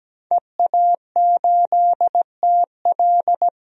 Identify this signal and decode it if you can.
Morse code, heard as a keyed tone, about 17 wpm, 700 Hz. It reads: EA8TL
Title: DXCC-100 Call Signs - 5 Characters - 17wpm